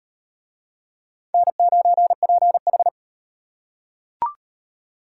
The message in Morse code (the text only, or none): N9PH